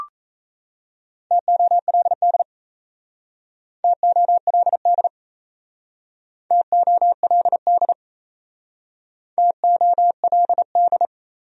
Morse code, heard as a keyed tone, about 37 wpm, 700 Hz. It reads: TOLD TOLD TOLD TOLD